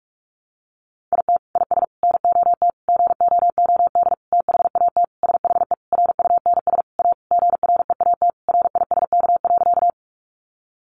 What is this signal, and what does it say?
Morse code, heard as a keyed tone, about 44 wpm, 700 Hz. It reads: IT IS NOT GOOD THAT SHE RUNS A GREAT RISK.
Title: Sentences from Top 600 Words 45wpm